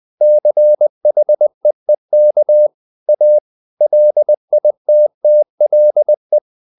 Morse code, heard as a keyed tone, about 20 wpm, 600 Hz. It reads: CHEEK A LITTLE
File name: the old man and the sea / the old man and the sea part 1 20wpm